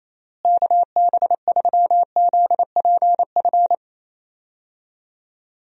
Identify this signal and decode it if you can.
Morse code, heard as a keyed tone, about 28 wpm, 700 Hz. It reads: KB3ZPF
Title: USA Call Signs - 6 Characters - Repeated 3x - 28wpm